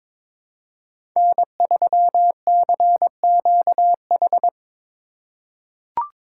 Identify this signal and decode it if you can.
Morse code, heard as a keyed tone, about 22 wpm, 700 Hz. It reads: N3CQH
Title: USA Call Signs - 5 Characters - 4x Character Spacing - 22wpm